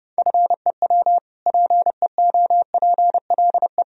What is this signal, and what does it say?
Morse code, heard as a keyed tone, about 30 wpm, 700 Hz. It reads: FEW PEOPLE
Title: Sentences from Top 1000 Words - Review Part 1 - 30wpm